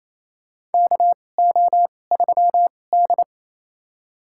Morse code, heard as a keyed tone, about 28 wpm, spaced as for 20 wpm, 700 Hz. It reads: KO3D
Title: USA Call Signs - 4 Characters - 2x Character Spacing - 28wpm